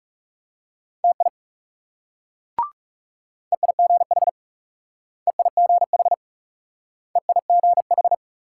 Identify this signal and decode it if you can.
Morse code, heard as a keyed tone, about 37 wpm, 700 Hz. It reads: TI EIGH EIGH EIGH
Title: Speed-Racing Multi-letter Phonemes - 30wpm